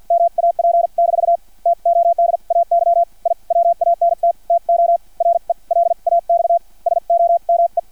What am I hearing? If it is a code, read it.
Morse code, heard as a keyed tone, about 50 wpm, spaced as for 41 wpm, 680 Hz. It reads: MAP- TODAY I WANT TO RELAX SOME
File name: archived / CRSnet